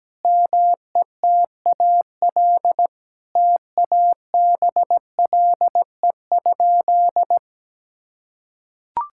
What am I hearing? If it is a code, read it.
Morse code, heard as a keyed tone, about 17 wpm, 700 Hz. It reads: METAL TABLE?
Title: Sentences from Top 1000 Words - Review Part 6 - 17wpm